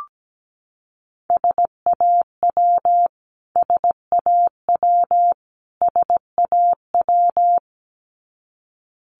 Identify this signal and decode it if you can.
Morse code, heard as a keyed tone, about 17 wpm, 700 Hz. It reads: SAW SAW SAW
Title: Top 500 Words Repeated 17wpm